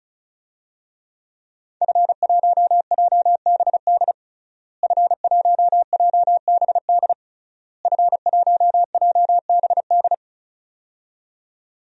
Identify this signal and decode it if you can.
Morse code, heard as a keyed tone, about 36 wpm, 700 Hz. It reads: F1JBD F1JBD F1JBD
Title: DXCC-100 Call Signs - 5 Characters - Repeated 3x - 35wpm